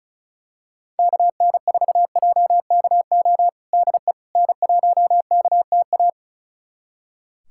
Morse code, heard as a keyed tone, about 34 wpm, 700 Hz. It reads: KN4JKO DE N1KTA
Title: USA Callsign DE Callsign 35wpm